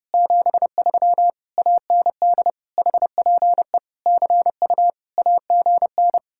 Morse code, heard as a keyed tone, about 30 wpm, 700 Hz. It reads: 73 AND HPE CU AGN